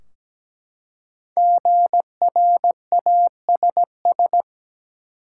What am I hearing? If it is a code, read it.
Morse code, heard as a keyed tone, about 17 wpm, 700 Hz. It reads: GRASS
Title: Five Letter Words - 17wpm